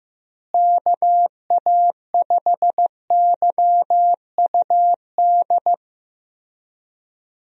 Morse code, heard as a keyed tone, about 15 wpm, 700 Hz. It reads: KA5YUD
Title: USA Call Signs - 6 Characters - Repeated 2x - 15wpm